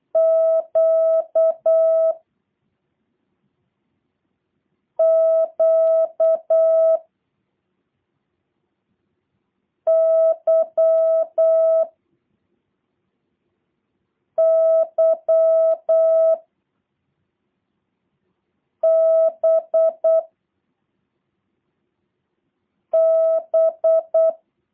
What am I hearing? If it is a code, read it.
Morse code, heard as a keyed tone, about 8 wpm, 650 Hz. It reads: Q Q Y Y B B
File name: jh8cbh / 2023cw / review1-6